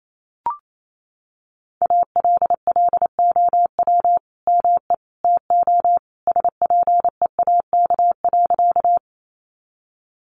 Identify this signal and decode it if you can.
Morse code, heard as a keyed tone, about 28 wpm, 700 Hz. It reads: ALLOW ME TO SPEAK.
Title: Sentences from Top 1100 Words 28wpm